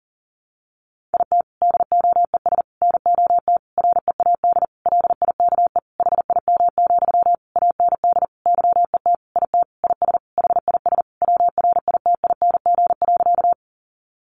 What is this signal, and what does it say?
Morse code, heard as a keyed tone, about 40 wpm, 700 Hz. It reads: IT DOES NOT READ LIKE HIM, AND YET IT IS HIS WRITING.
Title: Sentences from Top 800 Words 40wpm